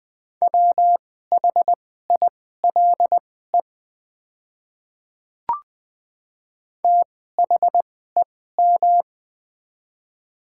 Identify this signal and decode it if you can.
Morse code, heard as a keyed tone, about 20 wpm, spaced as for 14 wpm, 700 Hz. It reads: WHILE THEM
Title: 2x Character Spacing - Top 300 Words - 20wpm